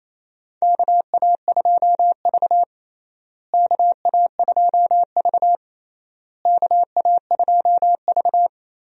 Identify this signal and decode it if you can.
Morse code, heard as a keyed tone, about 28 wpm, 700 Hz. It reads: KA2V KA2V KA2V
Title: USA Call Signs - 4 Characters - Repeated 3x - 28wpm